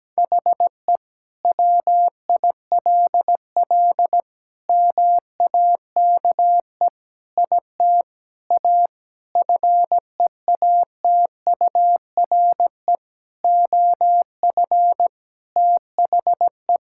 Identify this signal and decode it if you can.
Morse code, heard as a keyed tone, about 17 wpm, 700 Hz. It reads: HE WILL MAKE IT A FEATURE OF THE
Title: Sentences from Top 1000 Words - Review Part 5 - 17wpm